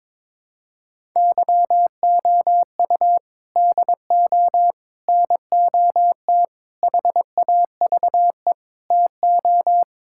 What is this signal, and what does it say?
Morse code, heard as a keyed tone, about 22 wpm, 700 Hz. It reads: YOU DO NOT HAVE TO
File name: practice / 22wpm / Sentences with 4-Word Phrases - 22wpm